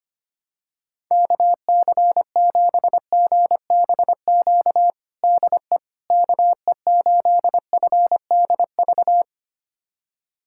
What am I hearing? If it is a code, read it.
Morse code, heard as a keyed tone, about 25 wpm, 700 Hz. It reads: KC7GBQ DE KE8FDV
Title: USA Callsign DE Callsign - 6 Characters - 25wpm